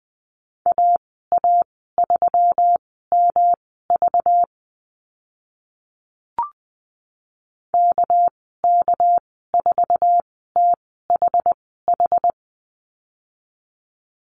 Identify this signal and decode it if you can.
Morse code, heard as a keyed tone, about 20 wpm, spaced as for 14 wpm, 700 Hz. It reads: AA3MV KK4THH
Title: 200 Lesson 32g - Letter k - Call Signs 20wpm